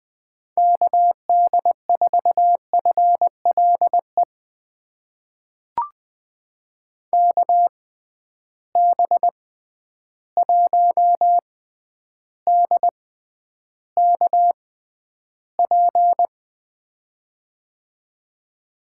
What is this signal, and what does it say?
Morse code, heard as a keyed tone, about 21 wpm, spaced as for 10 wpm, 700 Hz. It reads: KD4FLE KB1DKP
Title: USA Call Signs - 6x Character Spacing - 20wpm